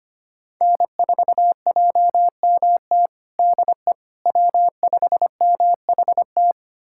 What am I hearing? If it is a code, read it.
Morse code, heard as a keyed tone, about 25 wpm, 700 Hz. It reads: N4JMT DE W5MHT
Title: USA Callsign DE Callsign - 5 Characters - 25wpm